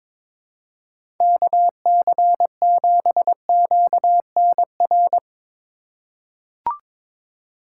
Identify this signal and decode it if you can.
Morse code, heard as a keyed tone, about 22 wpm, 700 Hz. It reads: KC7QNR
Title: USA Call Signs - 6 Characters - 4x Character Spacing - 22wpm